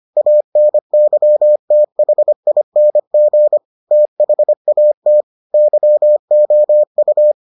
Morse code, heard as a keyed tone, about 25 wpm, 600 Hz. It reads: ANYTHING THAT YOU